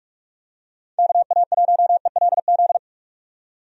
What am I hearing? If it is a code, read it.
Morse code, heard as a keyed tone, about 44 wpm, 700 Hz. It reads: KA1ELZ
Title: USA Call Signs - 6 Characters - 45wpm